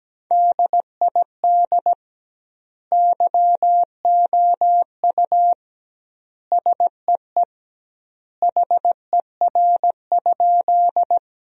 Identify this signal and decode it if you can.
Morse code, heard as a keyed tone, about 17 wpm, 700 Hz. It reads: DID YOU SEE HER?
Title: 2x Word-Spacing - Sentences from Top 100 Words - 17wpm